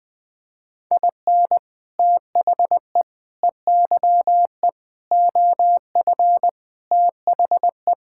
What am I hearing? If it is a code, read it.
Morse code, heard as a keyed tone, about 20 wpm, 700 Hz. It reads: IN THE EYE OF THE
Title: Sets of 6 Words - 20wpm